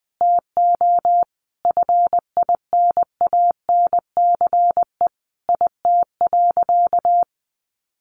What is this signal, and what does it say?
Morse code, heard as a keyed tone, about 20 wpm, 700 Hz. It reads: TO FINANCE IT.
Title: Sentences from Top 1500 Words 20wpm